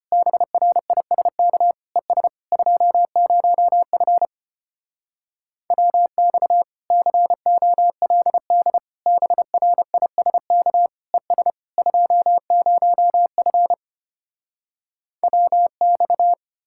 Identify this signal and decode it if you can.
Morse code, heard as a keyed tone, about 33 wpm, 700 Hz. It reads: BRISK ES 20F WX COLD BRISK ES 20F WX